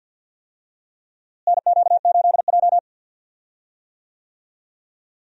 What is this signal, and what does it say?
Morse code, heard as a keyed tone, about 52 wpm, 700 Hz. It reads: NQ8J
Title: USA Call Signs - 4 Characters - Repeated 2x - 50wpm